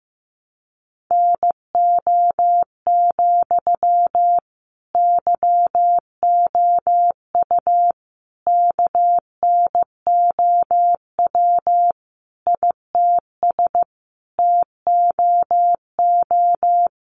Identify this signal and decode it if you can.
Morse code, heard as a keyed tone, about 15 wpm, 700 Hz. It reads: NO, YOU KNOW ITS TOO
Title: Sentences from Top 800 Words 15wpm